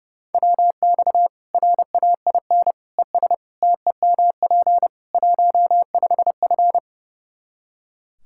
Morse code, heard as a keyed tone, about 30 wpm, 700 Hz. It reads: WX RAIN ES TEMP 15F
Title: Practice QSO - Exchange 2 - Info - 30wpm